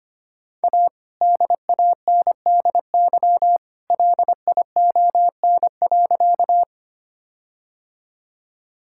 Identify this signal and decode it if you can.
Morse code, heard as a keyed tone, about 25 wpm, 700 Hz. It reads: A DANDY LION.
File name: practice / 25wpm / All-Age Appropriate Jokes - 25wpm